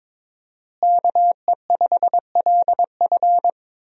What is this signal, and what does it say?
Morse code, heard as a keyed tone, about 22 wpm, 700 Hz. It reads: KE5LF
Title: USA Call Signs - 5 Characters - Repeated 2x - 22wpm